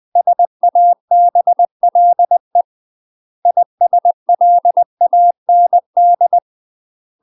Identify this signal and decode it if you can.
Morse code, heard as a keyed tone, about 20 wpm, 700 Hz. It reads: SABLE ISLAND
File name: UK MASTER QSOs TRAINING / 20WPM / UK MASTER QSOs 0067